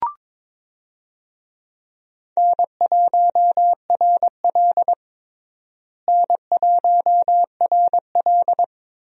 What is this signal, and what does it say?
Morse code, heard as a keyed tone, about 22 wpm, 700 Hz. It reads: N1RL N1RL
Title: USA Call Signs - 4 Characters - Repeated 2x - 22wpm